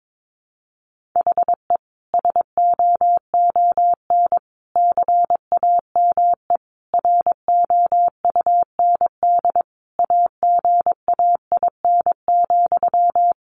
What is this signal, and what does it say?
Morse code, heard as a keyed tone, about 22 wpm, 700 Hz. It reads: HE SOON CAME ROUND AGAIN,
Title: Sentences from Top 600 Words 22wpm